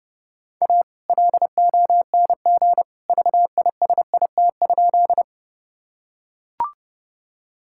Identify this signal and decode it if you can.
Morse code, heard as a keyed tone, about 30 wpm, 700 Hz. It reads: A LONG VISIT?
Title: Sentences from Top 1000 Words - Review Part 6 - 30wpm